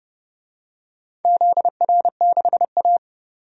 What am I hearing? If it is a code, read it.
Morse code, heard as a keyed tone, about 31 wpm, 700 Hz. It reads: ZR6A
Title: DXCC-100 Call Signs - 4 Characters - 30wpm